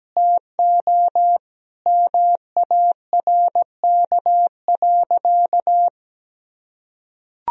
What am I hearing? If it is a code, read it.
Morse code, heard as a keyed tone, about 17 wpm, 700 Hz. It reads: TO MARK.
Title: Sentences from Top 1000 Words - Review Part 2 - 17wpm